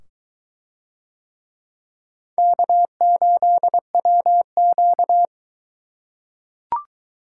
Morse code, heard as a keyed tone, about 22 wpm, 700 Hz. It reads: K8WQ